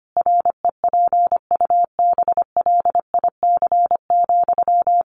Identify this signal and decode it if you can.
Morse code, heard as a keyed tone, about 25 wpm, 700 Hz. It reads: REPUBLIC,
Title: Sentences from Top 1500 Words 25wpm